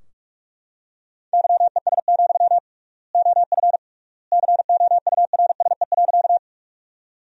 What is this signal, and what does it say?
Morse code, heard as a keyed tone, about 46 wpm, 700 Hz. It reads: YES, OF COURSE.
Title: 3x Word-Spacing - Sentences from Top 500 Words - 45wpm